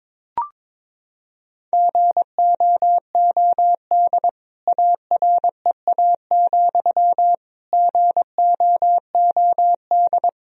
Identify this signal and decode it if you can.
Morse code, heard as a keyed tone, about 21 wpm, 700 Hz. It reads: GOOD AREA, GOOD
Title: Sentences from Top 1000 Words - Review Part 2 - 22wpm